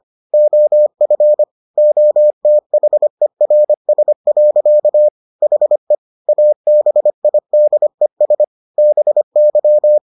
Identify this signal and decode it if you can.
Morse code, heard as a keyed tone, about 25 wpm, 600 Hz. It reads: OF OTHERS. HE ABIDES BY